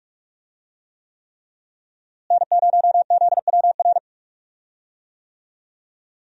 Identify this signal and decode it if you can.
Morse code, heard as a keyed tone, about 43 wpm, 700 Hz. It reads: N0ZWR
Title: USA Call Signs - 5 Characters - Repeated 3x - 45wpm